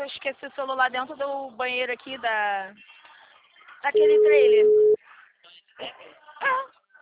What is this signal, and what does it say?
Ringback tone (Europe, 425 Hz), with speech.